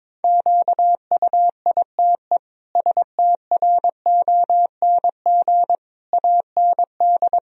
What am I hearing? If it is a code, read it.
Morse code, heard as a keyed tone, about 22 wpm, 700 Hz. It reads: QUITE STRONG AND
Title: Sentences from Top 1000 Words - Review Part 3 - 22wpm